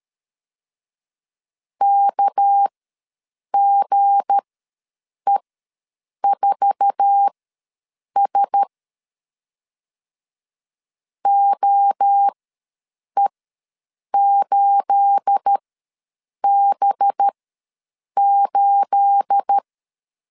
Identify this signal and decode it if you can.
Morse code, heard as a keyed tone, about 13 wpm, spaced as for 7 wpm, 800 Hz. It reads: KGE4S OE8B8
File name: LEZIONI / 9 28 ws 12wpm 3354